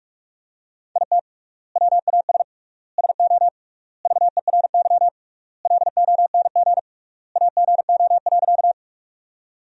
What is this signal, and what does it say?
Morse code, heard as a keyed tone, about 46 wpm, 700 Hz. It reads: IT WAS SO VERY LONG AGO.
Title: 3x Word-Spacing - Sentences from Top 500 Words - 45wpm